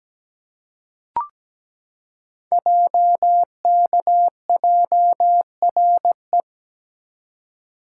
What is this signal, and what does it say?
Morse code, heard as a keyed tone, about 17 wpm, 700 Hz. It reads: JKJRE
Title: Code Groups - 5 Letters - 17wpm